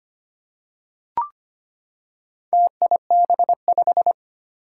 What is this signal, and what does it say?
Morse code, heard as a keyed tone, about 25 wpm, 700 Hz. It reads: TIB5
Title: Four Challenging Letter-Number Combinations 25wpm